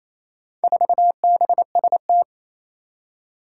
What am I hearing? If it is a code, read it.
Morse code, heard as a keyed tone, about 27 wpm, 700 Hz. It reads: 4BST
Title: Four Challenging Letter-Number Combinations 28wpm